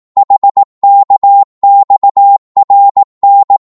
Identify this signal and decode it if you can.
Morse code, heard as a keyed tone, about 18 wpm, 800 Hz. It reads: HKXRN
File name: Morse / ltr-18wpm